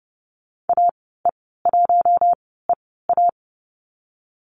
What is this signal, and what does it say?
Morse code, heard as a keyed tone, about 30 wpm, spaced as for 17 wpm, 700 Hz. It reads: AE1EA
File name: Lesson 14g - Number 2 - Call Signs 30wpm